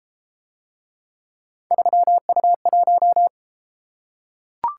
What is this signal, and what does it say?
Morse code, heard as a keyed tone, about 31 wpm, 700 Hz. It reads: 3U1